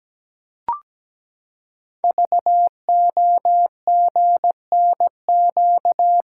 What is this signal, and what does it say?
Morse code, heard as a keyed tone, about 17 wpm, 700 Hz. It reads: VOGNQ
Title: Code Groups - 5 Letters - 17wpm